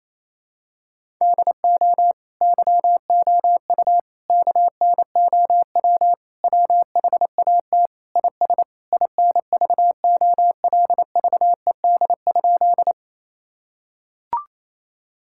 Morse code, heard as a keyed tone, about 28 wpm, 700 Hz. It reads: DO YOU KNOW WHAT IS INVOLVED?
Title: Sentences from Top 1000 Words - Review Part 5 - 28wpm